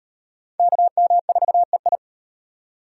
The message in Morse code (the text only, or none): KM4EI